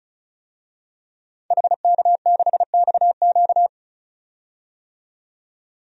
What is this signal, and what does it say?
Morse code, heard as a keyed tone, about 36 wpm, 700 Hz. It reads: HK6XQ
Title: DXCC-100 Call Signs - 5 Characters - 35wpm